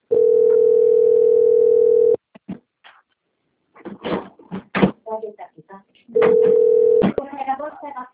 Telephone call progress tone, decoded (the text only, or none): ringback tone